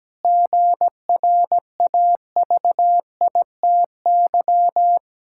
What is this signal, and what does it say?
Morse code, heard as a keyed tone, about 17 wpm, 700 Hz. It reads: GRAVITY
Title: Pangram Challenge v2 - 17wpm